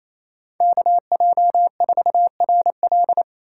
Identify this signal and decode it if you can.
Morse code, heard as a keyed tone, about 36 wpm, 700 Hz. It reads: KJ4RL